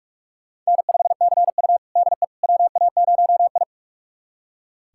Morse code, heard as a keyed tone, about 46 wpm, 690 Hz. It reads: N5CU DE WA0I
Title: USA Callsign DE Callsign - 4 Characters - 45wpm